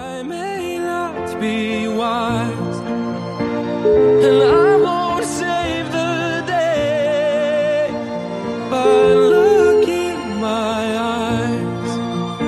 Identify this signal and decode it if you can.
Ringback tone (Europe, 425 Hz), with noise.